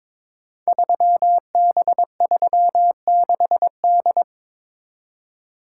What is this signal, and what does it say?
Morse code, heard as a keyed tone, about 21 wpm, 700 Hz. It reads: 3B36D
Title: Code Groups - 5 Letter:Numbers - 22wpm